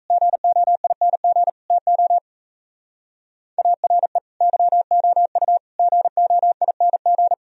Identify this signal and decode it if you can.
Morse code, heard as a keyed tone, about 34 wpm, 700 Hz. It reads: GOING TO ARE YOU GOING
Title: Speed-Racing - Sets of 4 Words 28wpm